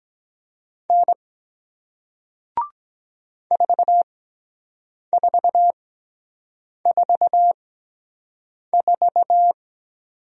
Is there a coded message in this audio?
Yes: Morse code from 0.9 to 9.5 s, heard as a keyed tone, about 23 wpm, spaced as for 7 wpm, 700 Hz. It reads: N 4444